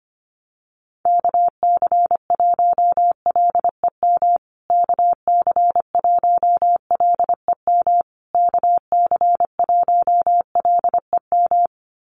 Morse code, heard as a keyed tone, about 25 wpm, 700 Hz. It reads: KC1LEM KC1LEM KC1LEM
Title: USA Call Signs Repeated 3x 25wpm